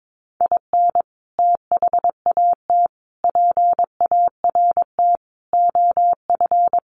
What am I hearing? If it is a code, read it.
Morse code, heard as a keyed tone, about 22 wpm, 700 Hz. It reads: IN THAT PART OF
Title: Sets of 5 Words Encore2 22wpm